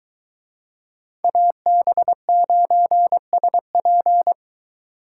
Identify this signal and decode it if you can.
Morse code, heard as a keyed tone, about 23 wpm, 700 Hz. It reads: AB9SP